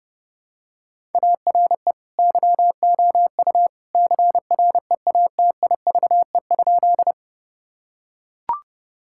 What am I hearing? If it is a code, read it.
Morse code, heard as a keyed tone, about 32 wpm, 700 Hz. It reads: ARE YOU CREATIVE?
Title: Sentences from Top 1000 Words - Review Part 5 - 30wpm